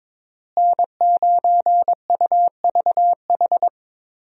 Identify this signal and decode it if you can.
Morse code, heard as a keyed tone, about 22 wpm, 700 Hz. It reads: N9UVH
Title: USA Call Signs - 5 Characters - Repeated 3x - 22wpm